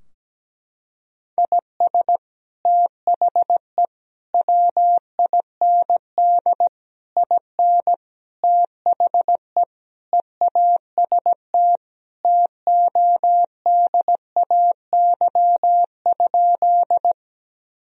Morse code, heard as a keyed tone, about 17 wpm, 700 Hz. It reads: IS THE WIND IN THE EAST TODAY?
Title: Sentences from Top 1000 Words - Review Part 3 - 17wpm